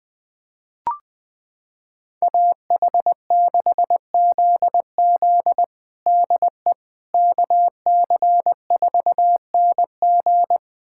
Morse code, heard as a keyed tone, about 21 wpm, 700 Hz. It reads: AH6ZZ DE KC4NG
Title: USA Callsign DE Callsign - 5 Characters - 20wpm